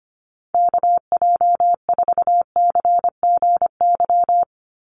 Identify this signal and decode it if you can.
Morse code, heard as a keyed tone, about 25 wpm, 700 Hz. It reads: KJ4CGY
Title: USA Call Signs 25wpm